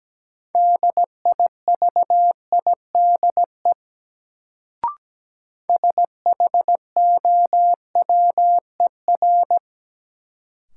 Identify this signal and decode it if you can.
Morse code, heard as a keyed tone, about 17 wpm, 700 Hz. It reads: DIVIDE SHOWER
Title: Sentences from Top 1600 Words - New Word Drill Alt - 17wpm